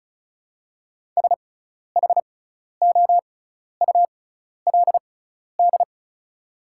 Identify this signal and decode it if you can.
Morse code, heard as a keyed tone, about 36 wpm, spaced as for 12 wpm, 700 Hz. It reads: SHOULD
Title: 6x Character Spacing - Top 300 Words - 35wpm